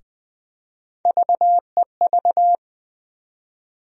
Morse code, heard as a keyed tone, about 20 wpm, 700 Hz. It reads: VEV
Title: USA Call Sign Postfixes - 3 Letters 20wpm